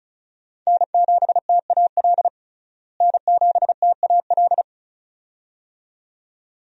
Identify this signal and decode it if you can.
Morse code, heard as a keyed tone, about 34 wpm, 700 Hz. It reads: N7TAL N7TAL
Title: USA Call Signs - 5 Characters - Repeated 2x - 35wpm